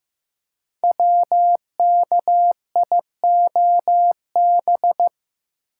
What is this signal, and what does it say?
Morse code, heard as a keyed tone, about 15 wpm, 700 Hz. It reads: WKIOB
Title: Code Groups - 5 Letters - 15wpm